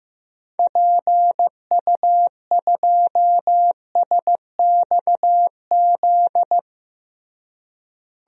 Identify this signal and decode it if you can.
Morse code, heard as a keyed tone, about 16 wpm, 700 Hz. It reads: PU2SXZ
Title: DXCC-100 Call Signs - 6 Characters - Repeated 2x - 15wpm